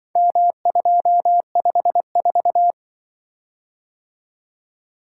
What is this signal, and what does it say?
Morse code, heard as a keyed tone, about 25 wpm, 700 Hz. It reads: M254
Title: Speed-Racing - Four Letter-Number Combinations - 20wpm